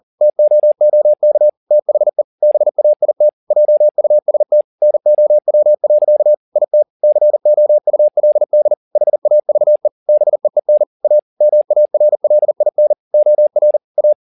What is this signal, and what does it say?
Morse code, heard as a keyed tone, about 40 wpm, 600 Hz. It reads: TOOK THE BAIT JUST NOW. IT COULD HAVE BEEN A MARLIN OR A